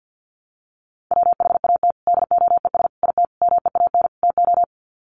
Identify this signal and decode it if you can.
Morse code, heard as a keyed tone, about 50 wpm, 700 Hz. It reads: WHAT DOES IT MEAN TO
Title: Sets of 5 Words Encore2 50wpm